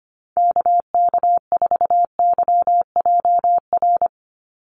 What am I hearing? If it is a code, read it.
Morse code, heard as a keyed tone, about 25 wpm, 700 Hz. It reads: KK4YJR
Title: USA Call Signs 25wpm